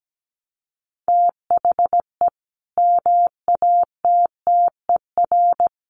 Morse code, heard as a keyed tone, about 17 wpm, 700 Hz. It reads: THE MATTER
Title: Sentences from Top 1200 Words 17wpm